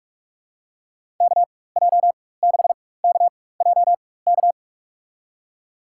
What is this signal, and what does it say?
Morse code, heard as a keyed tone, about 46 wpm, spaced as for 21 wpm, 700 Hz. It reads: KJ6KJK
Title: USA Call Signs - 6 Characters - 4x Character Spacing - 45wpm